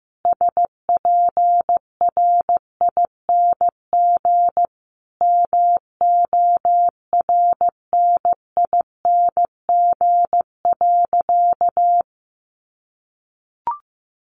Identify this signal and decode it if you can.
Morse code, heard as a keyed tone, about 15 wpm, 700 Hz. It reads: SPRING MORNING.
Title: Sentences from Top 1200 Words 15wpm